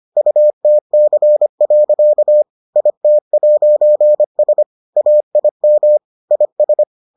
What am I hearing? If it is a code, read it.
Morse code, heard as a keyed tone, about 25 wpm, 600 Hz. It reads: UTC. IT'S AIM IS